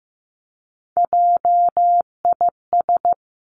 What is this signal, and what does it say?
Morse code, heard as a keyed tone, about 15 wpm, 700 Hz. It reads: JIS
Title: Three Challenging Letter-Number Combinations 15wpm